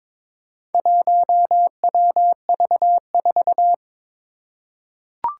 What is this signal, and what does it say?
Morse code, heard as a keyed tone, about 23 wpm, 700 Hz. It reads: 1WV4
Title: Four Challenging Letter-Number Combinations 22wpm